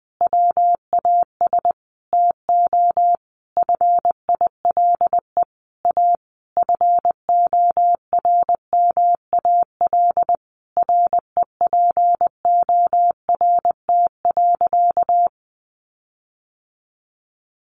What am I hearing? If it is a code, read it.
Morse code, heard as a keyed tone, about 20 wpm, 700 Hz. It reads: WAS TO FILE A FORMAL REPORT.
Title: Sentences from Top 1300 Words 20wpm